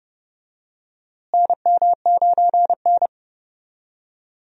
Morse code, heard as a keyed tone, about 30 wpm, 700 Hz. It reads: NM9N
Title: USA Call Signs - 4 Characters - Repeated 2x - 30wpm